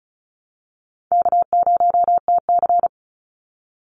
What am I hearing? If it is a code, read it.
Morse code, heard as a keyed tone, about 34 wpm, 700 Hz. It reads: K0TC
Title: Call Signs Repeated 3x - 35wpm